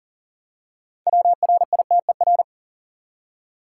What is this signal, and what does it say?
Morse code, heard as a keyed tone, about 40 wpm, 700 Hz. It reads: WRITER
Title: Six Letter Words - 40wpm